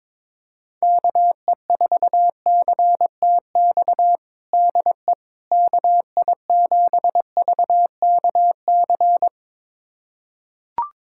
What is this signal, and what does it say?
Morse code, heard as a keyed tone, about 21 wpm, 700 Hz. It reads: KE4CTX DE KI7VKC
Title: USA Callsign DE Callsign - 6 Characters - 22wpm